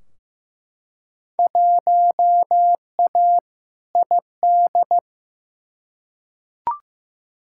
Morse code, heard as a keyed tone, about 15 wpm, 700 Hz. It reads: 1A ID
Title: Field Day Exchanges - 15wpm